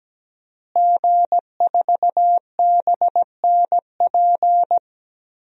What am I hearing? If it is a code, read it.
Morse code, heard as a keyed tone, about 17 wpm, 700 Hz. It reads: G4BNP
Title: DXCC-100 Call Signs - 5 Characters - Repeated 3x - 17wpm